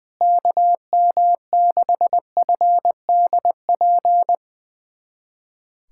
Morse code, heard as a keyed tone, about 21 wpm, 700 Hz. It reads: KM6FDP
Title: USA Callsign DE Callsign - 6 Characters - 20wpm